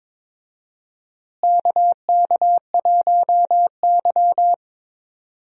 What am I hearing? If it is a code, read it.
Morse code, heard as a keyed tone, about 22 wpm, 700 Hz. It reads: KK1Y
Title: USA Call Signs - 4 Characters - Repeated 2x - 22wpm